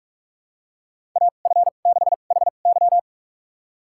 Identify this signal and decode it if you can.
Morse code, heard as a keyed tone, about 46 wpm, spaced as for 33 wpm, 700 Hz. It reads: AF6HY
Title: USA Call Signs - 5 Characters - 2x Character Spacing - 45wpm